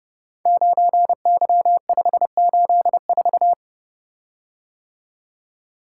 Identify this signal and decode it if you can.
Morse code, heard as a keyed tone, about 30 wpm, 700 Hz. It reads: 9Y584
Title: Code Groups - 5 Letter:Numbers - 30wpm